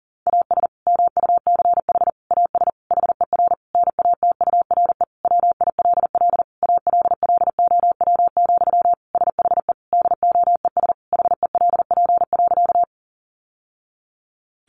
Morse code, heard as a keyed tone, about 40 wpm, 700 Hz. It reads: AS MUCH AS HER NATURE WILL ALLOW, SHE DOES HELP.
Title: Sentences from Top 700 Words 40wpm